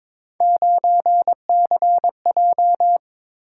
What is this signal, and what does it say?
Morse code, heard as a keyed tone, about 22 wpm, 700 Hz. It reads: 9CJ
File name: practice / 22wpm / Three Letter-Number Combinations - 22wpm